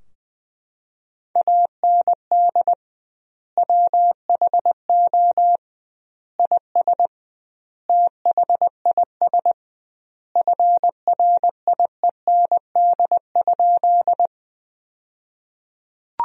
Morse code, heard as a keyed tone, about 20 wpm, 700 Hz. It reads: AND WHO IS THIS FRIEND?
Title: 2x Word-Spacing - Sentences from Top 300 Words - 20wpm